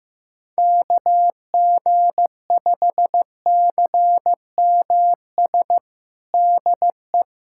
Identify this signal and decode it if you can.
Morse code, heard as a keyed tone, about 16 wpm, 700 Hz. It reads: KG5CMS DE
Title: USA Callsign DE Callsign - 6 Characters - 15wpm